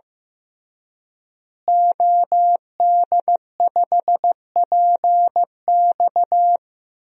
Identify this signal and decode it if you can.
Morse code, heard as a keyed tone, about 15 wpm, 700 Hz. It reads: OD5PX
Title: DXCC-100 Call Signs - 5 Characters - Repeated 3x - 15wpm